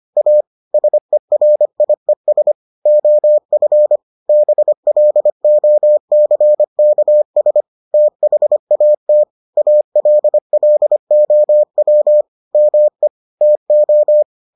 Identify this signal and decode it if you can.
Morse code, heard as a keyed tone, about 25 wpm, 600 Hz. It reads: A SERIES OF BLOCKS THAT ALLOW ME TO